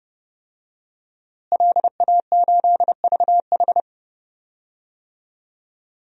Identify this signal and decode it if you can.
Morse code, heard as a keyed tone, about 32 wpm, 700 Hz. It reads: LA8VH
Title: DXCC-100 Call Signs - 5 Characters - Repeated 2x - 30wpm